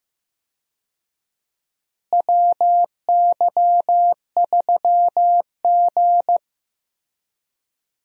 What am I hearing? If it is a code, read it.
Morse code, heard as a keyed tone, about 15 wpm, 700 Hz. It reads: WY3G
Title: USA Call Signs - 4 Characters - Repeated 3x - 15wpm